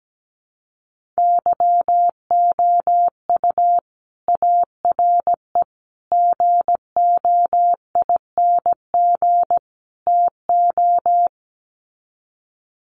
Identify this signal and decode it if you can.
Morse code, heard as a keyed tone, about 17 wpm, 700 Hz. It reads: YOU ARE GOING TO
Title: Sets of 4 Words 17wpm